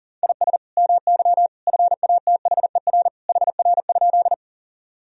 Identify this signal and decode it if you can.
Morse code, heard as a keyed tone, about 40 wpm, 700 Hz. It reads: IS MY FATHER HR?
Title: Sentences from Top 300 Words and Abbreviations 40wpm